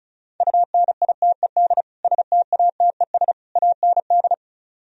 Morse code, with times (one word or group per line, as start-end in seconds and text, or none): 0.4-1.8 UNITED
2.0-3.3 STATES
3.6-4.3 AND